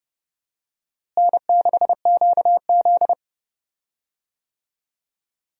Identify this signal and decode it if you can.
Morse code, heard as a keyed tone, about 31 wpm, 700 Hz. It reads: N6QZ
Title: USA Call Signs - 4 Characters - Repeated 2x - 30wpm